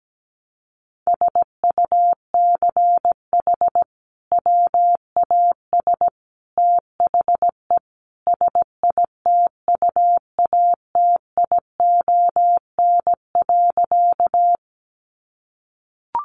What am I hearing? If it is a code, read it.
Morse code, heard as a keyed tone, about 17 wpm, 700 Hz. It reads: SUCH WAS THE SITUATION.
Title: Sentences from Top 1200 Words 17wpm